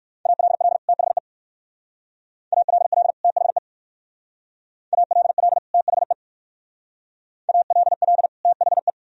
Morse code, heard as a keyed tone, about 61 wpm, 700 Hz. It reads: ALL THE ALL THE ALL THE ALL THE